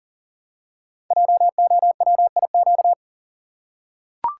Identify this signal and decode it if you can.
Morse code, heard as a keyed tone, about 38 wpm, 690 Hz. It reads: JOWIQ
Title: Code Groups - 5 Letters - 40wpm